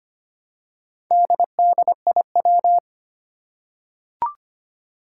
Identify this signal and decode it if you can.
Morse code, heard as a keyed tone, about 25 wpm, 700 Hz. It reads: DDIW
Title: Four Challenging Letter-Number Combinations 25wpm